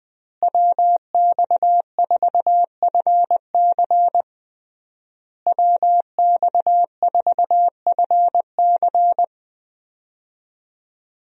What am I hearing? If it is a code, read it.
Morse code, heard as a keyed tone, about 20 wpm, 700 Hz. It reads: WX4FC WX4FC
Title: USA Call Signs - 5 Characters - Repeated 3x - 20wpm